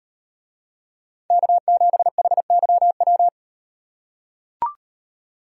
Morse code, heard as a keyed tone, about 37 wpm, 700 Hz. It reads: K7HYW